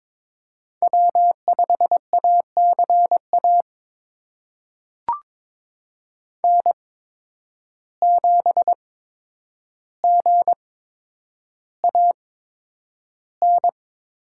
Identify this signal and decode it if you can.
Morse code, heard as a keyed tone, about 22 wpm, spaced as for 9 wpm, 700 Hz. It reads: W5ACA N7GAN
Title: USA Call Signs - 5 Characters - 8x Character Spacing - 22wpm